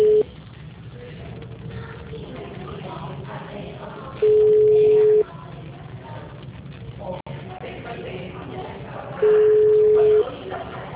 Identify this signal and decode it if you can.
Ringback tone (Europe, 425 Hz), with speech.